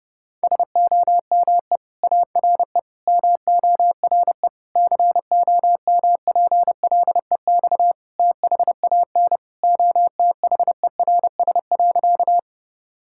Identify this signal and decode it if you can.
Morse code, heard as a keyed tone, about 30 wpm, 700 Hz. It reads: SOME ARE MORE COMPLEX THAN OTHERS.
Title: Sentences from Top 1000 Words - Review Part 6 - 30wpm